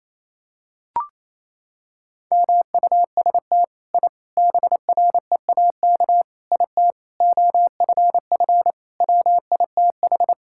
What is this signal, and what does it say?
Morse code, heard as a keyed tone, about 28 wpm, 700 Hz. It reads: MUST I BREAK IT OFF WITH
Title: Sentences from Top 1000 Words - Review Part 6 - 28wpm